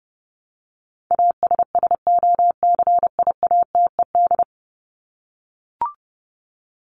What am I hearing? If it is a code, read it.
Morse code, heard as a keyed tone, about 30 wpm, 700 Hz. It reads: ASSOCIATED
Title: ED Suffix - Top 100 Words with Base Words 30wpm